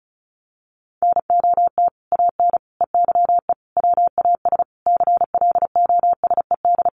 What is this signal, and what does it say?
Morse code, heard as a keyed tone, about 34 wpm, 700 Hz. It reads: NOT AN EYE WAS CLOSED
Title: Sentences from Top 1400 Words 35wpm